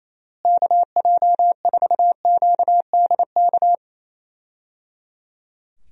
Morse code, heard as a keyed tone, about 28 wpm, 700 Hz. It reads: KJ4QDK
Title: USA Call Signs - 6 Characters - Repeated 3x - 28wpm